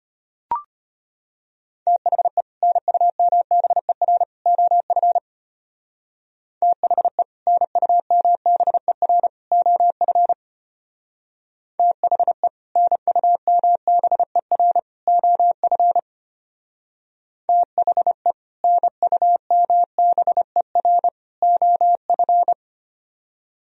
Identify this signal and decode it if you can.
Morse code, heard as a keyed tone, about 30 wpm, 700 Hz. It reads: THE NUMBER OF THE NUMBER OF THE NUMBER OF THE NUMBER OF
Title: Speed-Racing - Sets of 3 Words 25wpm